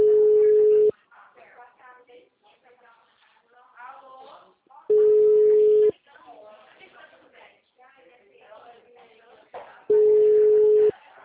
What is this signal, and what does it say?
Ringback tone (Europe, 425 Hz), with speech.
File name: out- -2009- - - .14800